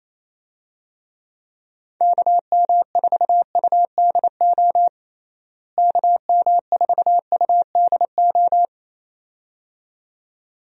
Morse code, heard as a keyed tone, about 28 wpm, 700 Hz. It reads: KM4UDO KM4UDO
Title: USA Call Signs - 6 Characters - Repeated 2x - 28wpm